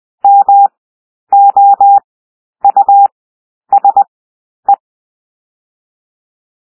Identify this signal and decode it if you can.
Morse code, heard as a keyed tone, about 20 wpm, 800 Hz. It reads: M O U S E
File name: cwmp3 / W1 SOUND WORDS 20 wpm